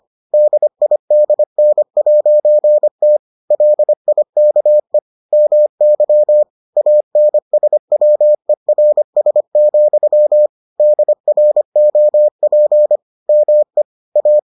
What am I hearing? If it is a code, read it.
Morse code, heard as a keyed tone, about 25 wpm, 600 Hz. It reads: DIDN'T LIKE MY ANSWERS, DROP ME A